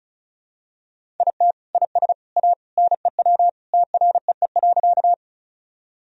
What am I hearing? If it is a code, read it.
Morse code, heard as a keyed tone, about 34 wpm, 700 Hz. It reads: IT IS A NEW TREE.
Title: Sentences from Top 1000 Words - Review Part 1 - 35wpm